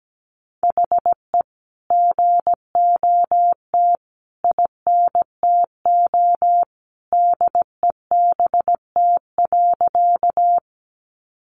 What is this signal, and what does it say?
Morse code, heard as a keyed tone, about 17 wpm, 700 Hz. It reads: HE GOT INTO DEBT.
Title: Sentences from Top 1300 Words 17wpm